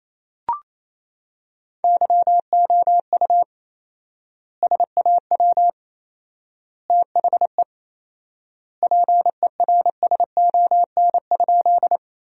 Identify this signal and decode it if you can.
Morse code, heard as a keyed tone, about 28 wpm, 700 Hz. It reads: YOU SAW THE PERSON?
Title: 4x Word-Spacing - Sentences from Top 400 Words - 28wpm